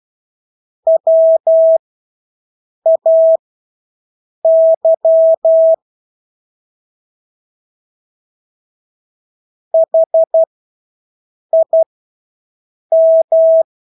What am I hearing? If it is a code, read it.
Morse code, heard as a keyed tone, about 12 wpm, spaced as for 6 wpm, 650 Hz. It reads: WAY HIM
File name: Engelske ord 12-6 wpm